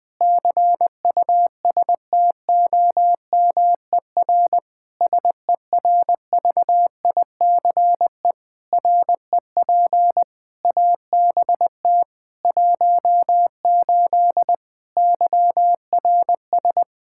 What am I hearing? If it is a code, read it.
Morse code, heard as a keyed tone, about 20 wpm, 700 Hz. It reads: CUSTOMER SERVICE REP ABT 18 YRS
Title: Practice QSO - Exchange 3 - More Info - 20wpm